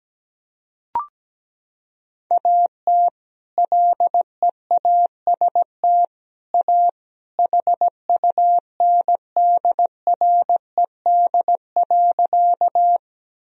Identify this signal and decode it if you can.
Morse code, heard as a keyed tone, about 17 wpm, 700 Hz. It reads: AT LEAST A HUNDRED.
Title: Sentences from Top 1000 Words - Review Part 2 - 17wpm